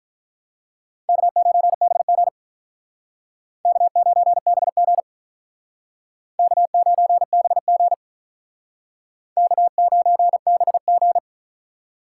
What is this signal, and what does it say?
Morse code, heard as a keyed tone, about 45 wpm, 700 Hz. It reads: K9BG K9BG K9BG K9BG